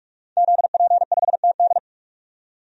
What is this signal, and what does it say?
Morse code, heard as a keyed tone, about 42 wpm, 700 Hz. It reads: ZP5TD
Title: DXCC-100 Call Signs - 5 Characters - 45wpm